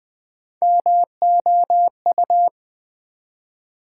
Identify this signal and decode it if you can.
Morse code, heard as a keyed tone, about 20 wpm, 700 Hz. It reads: MOU
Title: Build A Word - 1 Letter At A Time - 5-Letter Words - 20wpm